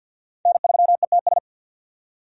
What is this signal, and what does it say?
Morse code, heard as a keyed tone, about 50 wpm, 700 Hz. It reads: N3ETS
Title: USA Call Signs - 5 Characters - 50wpm